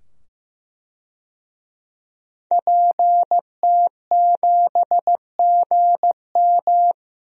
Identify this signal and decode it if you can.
Morse code, heard as a keyed tone, about 15 wpm, 700 Hz. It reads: PT7GM
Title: DXCC-100 Call Signs - 5 Characters - 15wpm